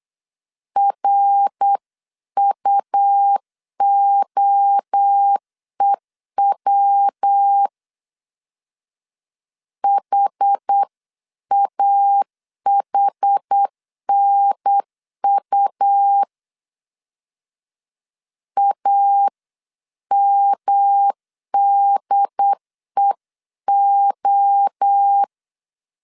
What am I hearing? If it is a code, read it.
Morse code, heard as a keyed tone, about 8 wpm, 800 Hz. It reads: RUOEW HAHNU AMDEO